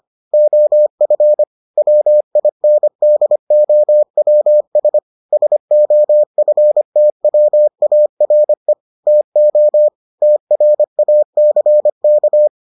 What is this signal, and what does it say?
Morse code, heard as a keyed tone, about 25 wpm, 600 Hz. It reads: OF WINDOWS SOFTWARE TO TRACK